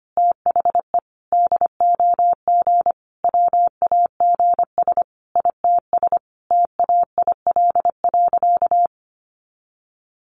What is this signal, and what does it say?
Morse code, heard as a keyed tone, about 25 wpm, 700 Hz. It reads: THE DOG WAGS ITS TAIL.
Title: Easy Phrases 25wpm